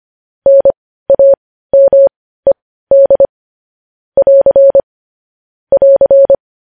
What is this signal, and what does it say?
Morse code, heard as a keyed tone, about 25 wpm, spaced as for 15 wpm, 550 Hz. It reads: NAMED <AR> <AR>
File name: podcast / News headlines-1x-25x15